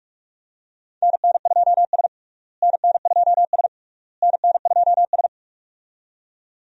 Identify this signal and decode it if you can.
Morse code, heard as a keyed tone, about 46 wpm, 700 Hz. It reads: NN2S NN2S NN2S
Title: USA Call Signs - 4 Characters - Repeated 3x - 45wpm